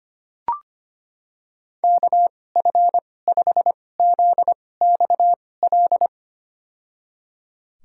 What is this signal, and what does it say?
Morse code, heard as a keyed tone, about 25 wpm, spaced as for 18 wpm, 700 Hz. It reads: KF5ZXL
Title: USA Call Signs - 6 Characters - 2x Character Spacing - 25wpm